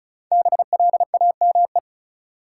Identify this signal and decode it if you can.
Morse code, heard as a keyed tone, about 36 wpm, 700 Hz. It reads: BLAME
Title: Five Letter Words - 35wpm